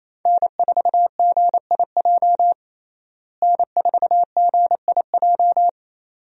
Morse code, heard as a keyed tone, about 28 wpm, 700 Hz. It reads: N4GIJ N4GIJ
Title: USA Call Signs - 5 Characters - Repeated 2x - 28wpm